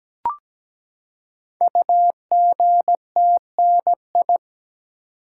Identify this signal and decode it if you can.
Morse code, heard as a keyed tone, about 17 wpm, 700 Hz. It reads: UGTNI
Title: Code Groups - 5 Letters - 17wpm